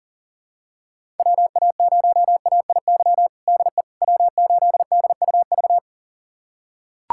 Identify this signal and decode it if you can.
Morse code, heard as a keyed tone, about 37 wpm, 700 Hz. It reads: WA0AIY DE W8DUV
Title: USA Callsign DE Callsign 40wpm